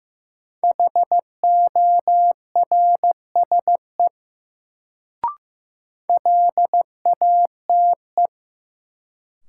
Morse code, heard as a keyed tone, about 16 wpm, 700 Hz. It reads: HORSE LATE
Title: Sentences from Top 300 Words - New Word Drill - 15wpm